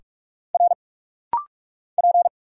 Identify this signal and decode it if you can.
Morse code, heard as a keyed tone, about 46 wpm, 700 Hz. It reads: R P
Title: Single Letters - Rapid-Fire 45wpm